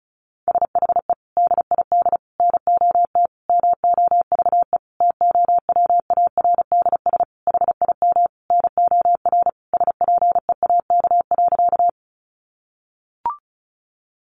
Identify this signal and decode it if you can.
Morse code, heard as a keyed tone, about 34 wpm, 700 Hz. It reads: SHE DID NOT MOVE TOWARDS HIM NOR SPEAK.
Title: Sentences from Top 1500 Words 35wpm